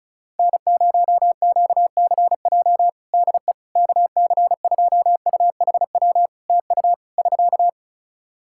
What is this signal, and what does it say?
Morse code, heard as a keyed tone, about 34 wpm, 700 Hz. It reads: N0QCJ DE KC2UHW TU <SK>
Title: K1USN SST QSO Practice 35wpm